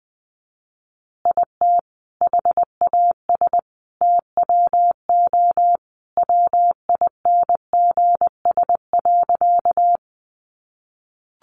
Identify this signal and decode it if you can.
Morse code, heard as a keyed tone, about 20 wpm, 700 Hz. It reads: IT HAS TWO WINGS.
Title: Additional Easy Phrases 20wpm